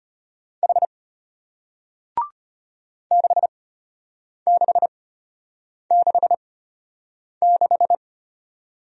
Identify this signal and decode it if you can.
Morse code, heard as a keyed tone, about 33 wpm, spaced as for 8 wpm, 700 Hz. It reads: H 6666